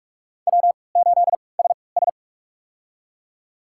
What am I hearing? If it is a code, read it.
Morse code, heard as a keyed tone, about 44 wpm, spaced as for 25 wpm, 700 Hz. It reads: W8SS
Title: USA Call Signs - 4 Characters - 3x Character Spacing - 45wpm